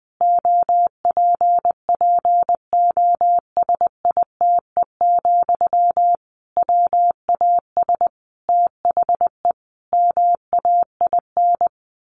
Morse code, heard as a keyed tone, about 20 wpm, 700 Hz. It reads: OPPOSITE, WAS THE MAIN
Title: Sentences from Top 1300 Words 20wpm